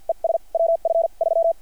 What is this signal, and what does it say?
Morse code, heard as a keyed tone, about 50 wpm, spaced as for 42 wpm, 660 Hz. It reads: ES WF 3
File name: archived / CRSnet